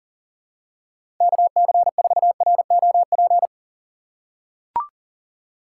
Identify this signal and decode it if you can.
Morse code, heard as a keyed tone, about 40 wpm, 700 Hz. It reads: KC4ROP
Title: USA Call Signs - 6 Characters - 4x Character Spacing - 40wpm